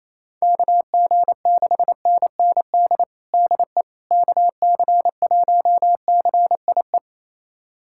Morse code, heard as a keyed tone, about 28 wpm, 700 Hz. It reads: KG6NND DE KC1CIE
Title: USA Callsign DE Callsign - 6 Characters - 28wpm